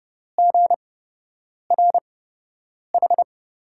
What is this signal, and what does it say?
Morse code, heard as a keyed tone, about 30 wpm, spaced as for 8 wpm, 700 Hz. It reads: GRH
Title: USA Call Signs - 5 Characters - 8x Character Spacing - 30wpm